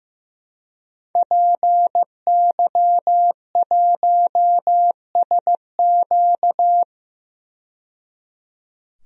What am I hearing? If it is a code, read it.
Morse code, heard as a keyed tone, about 15 wpm, 700 Hz. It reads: PY1SQ
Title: DXCC-100 Call Signs - 5 Characters - 15wpm